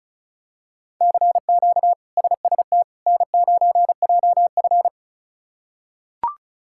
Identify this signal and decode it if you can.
Morse code, heard as a keyed tone, about 34 wpm, 700 Hz. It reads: CQ SST N9JF
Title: K1USN SST QSO Practice - 35wpm